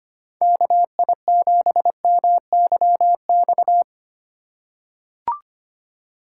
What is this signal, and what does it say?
Morse code, heard as a keyed tone, about 25 wpm, 700 Hz. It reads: KI7MYX
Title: USA Call Signs - 6 Characters - 3x Character Spacing - 25wpm